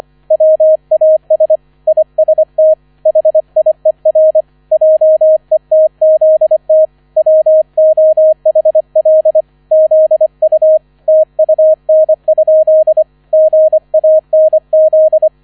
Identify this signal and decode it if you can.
Morse code, heard as a keyed tone, about 25 wpm, 630 Hz, with radio noise. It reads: WAS IST HIER JETZT WOHL ZU TUN? GANZ
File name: 001 Erster Streich 25wpm